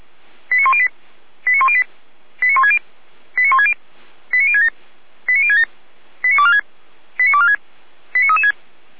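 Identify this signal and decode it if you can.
Five-tone selective call (ZVEI), heard as quick runs of five tones, tones of about 70 ms. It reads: 89108 89108 89160 89160 89067 89067 89246 89246 89336
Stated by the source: Five-tone ZVEI selective call 89078, received by SDR.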